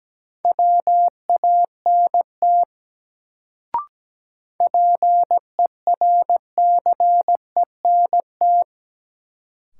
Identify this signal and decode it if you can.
Morse code, heard as a keyed tone, about 17 wpm, 700 Hz. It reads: WANT PERCENT
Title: Sets of 2 Words - New Word Drill - 17wpm